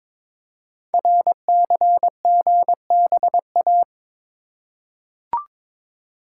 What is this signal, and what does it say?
Morse code, heard as a keyed tone, about 23 wpm, 700 Hz. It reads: RCGBA
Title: Code Groups - 5 Letters - 22wpm